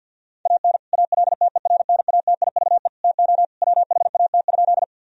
Morse code, heard as a keyed tone, about 50 wpm, 700 Hz. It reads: AN ALTERNATIVE TO WHAT?
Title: Sentences from Top 1000 Words - Review Part 4 - 50wpm